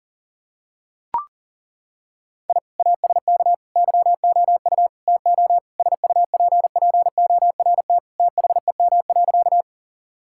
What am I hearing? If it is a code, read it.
Morse code, heard as a keyed tone, about 37 wpm, 700 Hz. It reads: I ASK YOU TO SUPPORT THEM.
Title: Sentences from Top 1000 Words - Review Part 5 - 40wpm